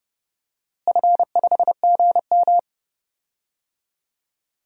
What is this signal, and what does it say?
Morse code, heard as a keyed tone, about 29 wpm, 700 Hz. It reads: F5GM
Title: DXCC-100 Call Signs - 4 Characters - Repeated 3x - 30wpm